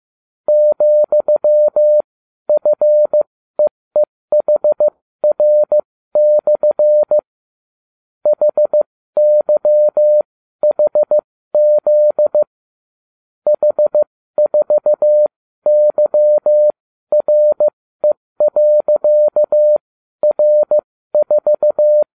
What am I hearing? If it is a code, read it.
Morse code, heard as a keyed tone, about 15 wpm, spaced as for 12 wpm, 600 Hz. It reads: ,FEEHR/ HYHZ H4YRE.R4